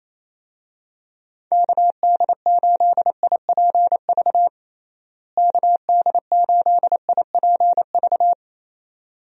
Morse code, heard as a keyed tone, about 28 wpm, 700 Hz. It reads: KD8IPV KD8IPV
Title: USA Call Signs - 6 Characters - Repeated 2x - 28wpm